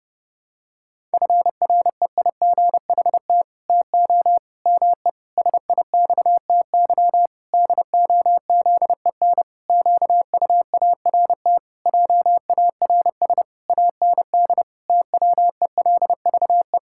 Morse code, heard as a keyed tone, about 30 wpm, 700 Hz. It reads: FREIGHT TO ME SIXTY DOZEN QUART JARS AND TWELVE
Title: Pangram Challenge v2 - 30wpm